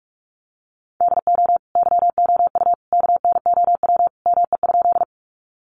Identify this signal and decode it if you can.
Morse code, heard as a keyed tone, about 44 wpm, 700 Hz. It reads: DO YOU KNOW ME?
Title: Sentences from Top 200 Words 45wpm